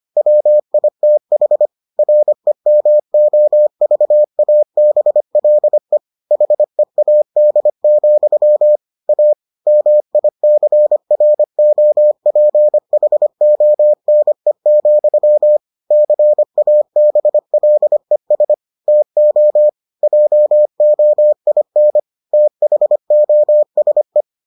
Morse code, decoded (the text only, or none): WITH REMOVABLE HEAD, A MICROPHONE, CABLES TO JOIN THOSE